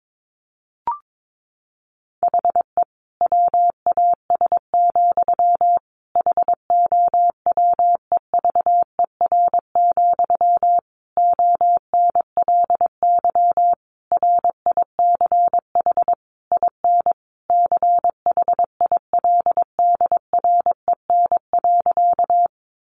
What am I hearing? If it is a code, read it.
Morse code, heard as a keyed tone, about 22 wpm, 700 Hz. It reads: HE WAS, HOWEVER, ONLY RICH IN CHILDREN.
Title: Sentences from Top 1100 Words 22wpm